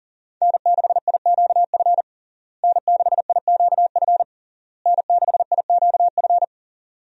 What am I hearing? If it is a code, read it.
Morse code, heard as a keyed tone, about 40 wpm, 700 Hz. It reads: N6IQF N6IQF N6IQF
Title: USA Call Signs - 5 Characters - Repeated 3x - 40wpm